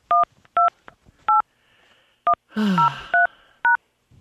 DTMF, heard as telephone touch tones, tones of about 140 ms. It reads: 128103#